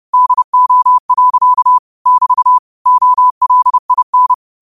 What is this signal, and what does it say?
Morse code, heard as a keyed tone, about 30 wpm, 1000 Hz. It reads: DO. <BT> OLIN